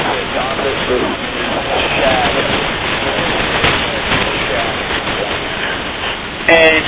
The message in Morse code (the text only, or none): none